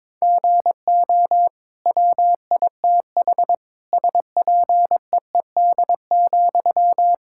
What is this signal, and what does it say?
Morse code, heard as a keyed tone, about 22 wpm, 700 Hz. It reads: GO WITH SPEED,
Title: Sentences from Top 1000 Words - Review Part 5 - 22wpm